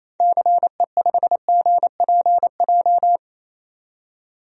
Morse code, heard as a keyed tone, about 28 wpm, 700 Hz. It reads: CE5GPJ
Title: DXCC-100 Call Signs - 6 Characters - 28wpm